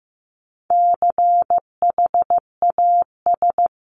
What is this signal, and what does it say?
Morse code, heard as a keyed tone, about 15 wpm, 700 Hz. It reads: CHAS
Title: CWT Exchange 15wpm